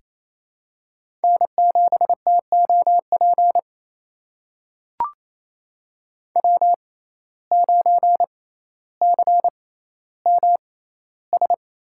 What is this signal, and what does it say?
Morse code, heard as a keyed tone, about 28 wpm, spaced as for 14 wpm, 700 Hz. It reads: N7TOP W9CMS
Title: USA Call Signs - 5 Characters - 6x Character Spacing - 28wpm